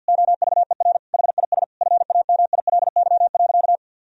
Morse code, heard as a keyed tone, about 52 wpm, 700 Hz. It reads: OVER HIS FAMILY.
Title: Sentences from Top 1000 Words - Review Part 3 - 50wpm